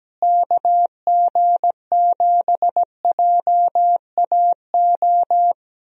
Morse code, heard as a keyed tone, about 17 wpm, 700 Hz. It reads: KG7JAO
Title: USA Call Signs - 6 Characters - Repeated 3x - 17wpm